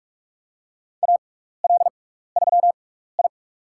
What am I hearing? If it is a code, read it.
Morse code, heard as a keyed tone, about 43 wpm, spaced as for 15 wpm, 700 Hz. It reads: AL3I
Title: USA Call Signs - 4 Characters - 6x Character Spacing - 45wpm